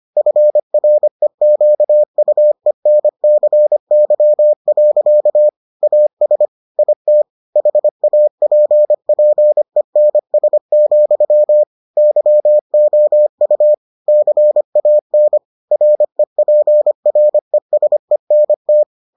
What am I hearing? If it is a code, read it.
Morse code, heard as a keyed tone, about 25 wpm, 600 Hz. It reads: FREQUENCY. AS IT HAPPENS, YOU CAN REPRESENT